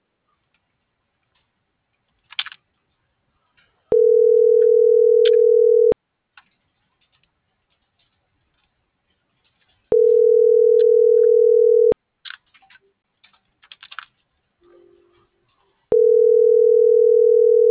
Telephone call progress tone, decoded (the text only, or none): ringback tone